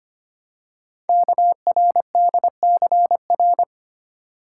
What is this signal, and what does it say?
Morse code, heard as a keyed tone, about 25 wpm, 700 Hz. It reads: KRDCR
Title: Code Groups - 5 Letters - 25wpm